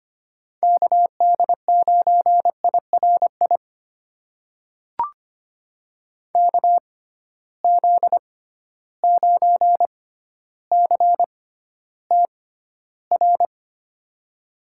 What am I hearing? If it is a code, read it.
Morse code, heard as a keyed tone, about 25 wpm, spaced as for 13 wpm, 700 Hz. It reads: KD9IRI KZ9CTR
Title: USA Call Signs - 6 Characters - 6x Character Spacing - 25wpm